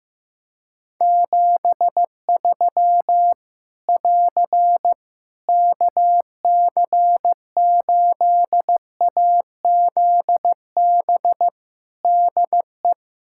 Morse code, heard as a keyed tone, about 15 wpm, 700 Hz. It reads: 73 <AR> KC8AZB DE
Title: Practice QSO - Exchange 4 - Ending - 15wpm